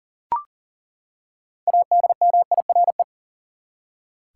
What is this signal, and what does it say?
Morse code, heard as a keyed tone, about 40 wpm, 700 Hz. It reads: ADMIRE
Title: Six Letter Words - 40wpm